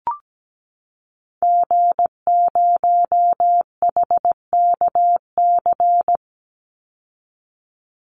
Morse code, heard as a keyed tone, about 17 wpm, 700 Hz. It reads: G0HKC
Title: Call Signs Repeated 2x - 17wpm